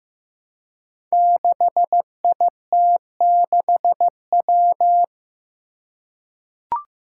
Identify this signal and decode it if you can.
Morse code, heard as a keyed tone, about 15 wpm, 700 Hz. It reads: 6IT6W
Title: Five Challenging Letter-Number Combinations 15wpm